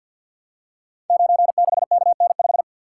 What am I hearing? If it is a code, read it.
Morse code, heard as a keyed tone, about 47 wpm, 700 Hz. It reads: 96KN5
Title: Code Groups - 5 Letter:Numbers - 50wpm